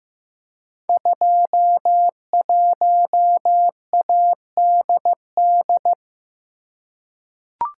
Morse code, heard as a keyed tone, about 15 wpm, 700 Hz. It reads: 21ADD
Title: Code Groups - 5 Letter:Numbers - 15wpm